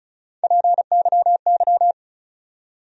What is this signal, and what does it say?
Morse code, heard as a keyed tone, about 36 wpm, 700 Hz. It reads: PYY
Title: USA Call Sign Postfixes - 3 Letters 35wpm